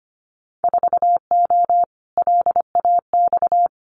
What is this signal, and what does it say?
Morse code, heard as a keyed tone, about 25 wpm, 700 Hz. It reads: 4O LAX
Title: Winter Field Day Exchanges Encore 25wpm